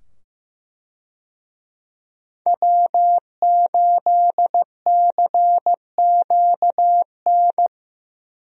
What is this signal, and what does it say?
Morse code, heard as a keyed tone, about 15 wpm, 700 Hz. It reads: W8CQN
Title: USA Call Signs - 5 Characters - 15wpm